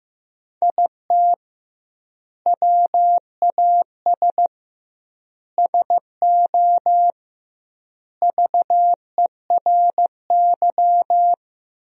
Morse code, heard as a keyed tone, about 15 wpm, 700 Hz. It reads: IT WAS SO VERY
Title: 2x Word-Spacing - Sentences from Top 500 Words - 15wpm